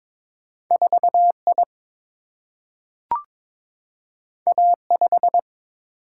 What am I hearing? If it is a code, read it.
Morse code, heard as a keyed tone, about 22 wpm, 700 Hz. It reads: 4I A5
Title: Single-Letter Single-Number 22wpm